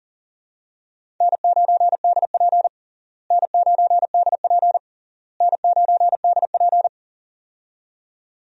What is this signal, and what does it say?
Morse code, heard as a keyed tone, about 40 wpm, 700 Hz. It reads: N9DP N9DP N9DP
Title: USA Call Signs - 4 Characters - Repeated 3x - 40wpm